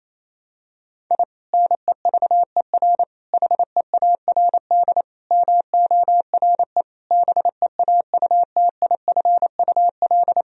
Morse code, heard as a keyed tone, about 28 wpm, 700 Hz. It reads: I NEVER HEARD MORE BEAUTIFUL
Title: Sentences from Top 1000 Words - Review Part 1 - 28wpm